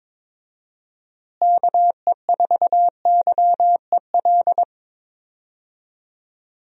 Morse code, heard as a keyed tone, about 21 wpm, 700 Hz. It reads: KE4YEL
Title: USA Call Signs - 6 Characters - Repeated 3x - 22wpm